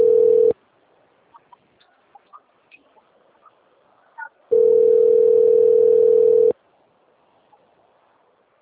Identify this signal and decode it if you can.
Ringback tone (North America), 440 + 480 Hz.